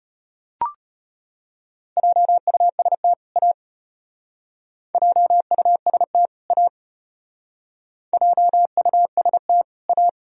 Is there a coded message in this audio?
Yes: Morse code from 2.0 to 10.1 s, heard as a keyed tone, about 34 wpm, 700 Hz. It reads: JUST A JUST A JUST A